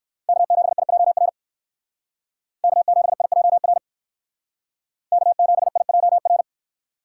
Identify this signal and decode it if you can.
Morse code, heard as a keyed tone, about 58 wpm, 700 Hz. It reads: K7IJR K7IJR K7IJR